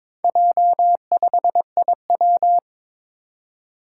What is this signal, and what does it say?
Morse code, heard as a keyed tone, about 22 wpm, 700 Hz. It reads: J5IW
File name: practice / 22wpm / Four Letter-Number Combinations - 22wpm